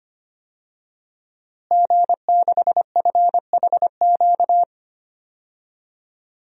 Morse code, heard as a keyed tone, about 25 wpm, 700 Hz. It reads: G6FHQ
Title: DXCC-100 Call Signs - 5 Characters - Repeated 3x - 25wpm